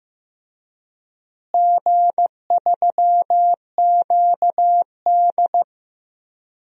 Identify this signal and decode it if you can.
Morse code, heard as a keyed tone, about 15 wpm, 700 Hz. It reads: G3QD
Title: DXCC-100 Call Signs - 4 Characters - Repeated 3x - 15wpm